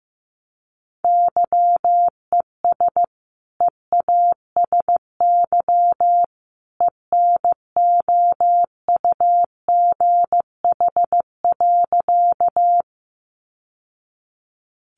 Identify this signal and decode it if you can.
Morse code, heard as a keyed tone, about 15 wpm, 700 Hz. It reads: YES EASY ENOUGH.
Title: Sentences from Top 700 Words 15wpm